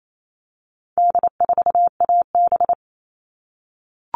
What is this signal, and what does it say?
Morse code, heard as a keyed tone, about 28 wpm, 700 Hz. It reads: D4AB
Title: All Cut Numbers - 4-Digit 28wpm